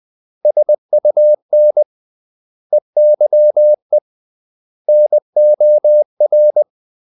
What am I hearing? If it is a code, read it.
Morse code, heard as a keyed tone, about 20 wpm, 600 Hz. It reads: SUN EYE NOR
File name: CW / Practice / 500MostCommonEnglishWords20